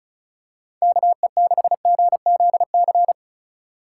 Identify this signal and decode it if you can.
Morse code, heard as a keyed tone, about 36 wpm, 700 Hz. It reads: KE6GZC
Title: USA Call Signs - 6 Characters - 2x Character Spacing - 35wpm